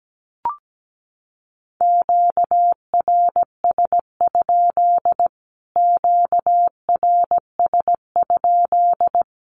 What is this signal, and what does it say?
Morse code, heard as a keyed tone, about 17 wpm, 700 Hz. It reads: QRS? QRS?
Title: Q-Codes Repeated 3x 17wpm